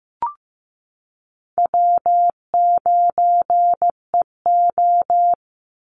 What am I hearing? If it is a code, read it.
Morse code, heard as a keyed tone, about 15 wpm, 700 Hz. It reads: W9EO
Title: Easy Call Signs - 15wpm